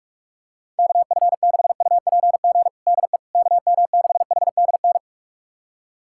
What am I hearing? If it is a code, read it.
Morse code, heard as a keyed tone, about 46 wpm, 700 Hz. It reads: KF6UPG DE KM6HDN
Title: USA Callsign DE Callsign - 6 Characters - 45wpm